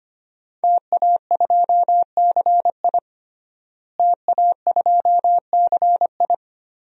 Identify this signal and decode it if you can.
Morse code, heard as a keyed tone, about 25 wpm, 700 Hz. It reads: TA2CI TA2CI
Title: DXCC-100 Call Signs - 5 Characters - Repeated 2x - 25wpm